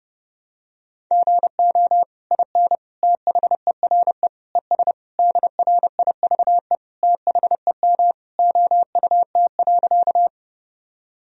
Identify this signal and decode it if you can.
Morse code, heard as a keyed tone, about 30 wpm, 700 Hz. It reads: GO IN THERE ES DRIVE THEM OUT.
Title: Sentences from Top 500 Words and Abbreviations 30wpm